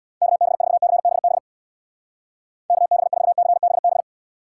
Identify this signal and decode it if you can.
Morse code, heard as a keyed tone, about 63 wpm, 700 Hz. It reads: KB4CBB KB4CBB